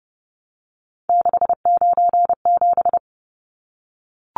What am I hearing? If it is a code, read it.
Morse code, heard as a keyed tone, about 30 wpm, 700 Hz. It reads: 697
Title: Numbers - 3-Digit 30wpm